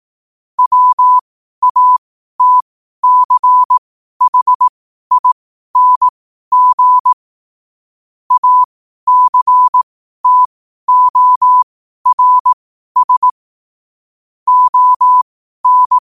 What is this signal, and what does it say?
Morse code, heard as a keyed tone, about 18 wpm, spaced as for 12 wpm, 1000 Hz. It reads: WATCHING ACTORS ON
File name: qotdfiles / 12wpm